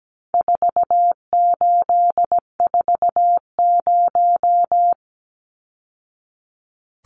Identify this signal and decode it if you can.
Morse code, heard as a keyed tone, about 17 wpm, 700 Hz. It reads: 4840
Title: Numbers - 4-Digit 17wpm